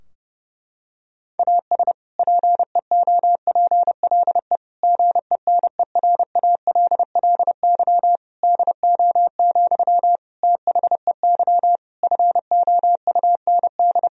Morse code, heard as a keyed tone, about 30 wpm, 700 Hz. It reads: AS PEOPLE GENERALLY DO, THEY FOUND
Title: Sentences from Top 1000 Words - Review Part 4 - 30wpm